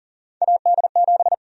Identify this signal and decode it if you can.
Morse code, heard as a keyed tone, about 39 wpm, 700 Hz. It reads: AD7
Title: USA Call Sign Prefixes - 40wpm